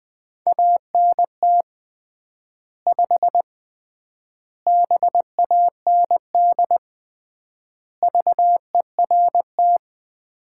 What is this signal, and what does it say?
Morse code, heard as a keyed tone, about 20 wpm, 700 Hz. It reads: ANT 5 BAND VERT
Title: Common QSO Phrases - 3x Word-Spacing - 20wpm